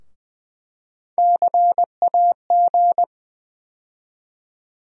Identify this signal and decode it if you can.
Morse code, heard as a keyed tone, about 21 wpm, 700 Hz. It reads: CAG
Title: USA Call Sign Postfixes - 3 Letters 20wpm